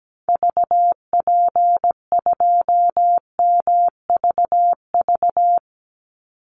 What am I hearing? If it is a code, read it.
Morse code, heard as a keyed tone, about 17 wpm, 700 Hz. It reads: VP2MVV
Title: Call Signs Repeated 3x - 17wpm